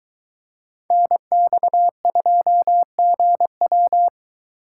Morse code, heard as a keyed tone, about 21 wpm, 700 Hz. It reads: NX2GW